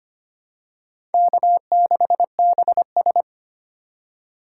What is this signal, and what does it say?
Morse code, heard as a keyed tone, about 25 wpm, 700 Hz. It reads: K6BS
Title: USA Call Signs - 4 Characters - 4x Character Spacing - 25wpm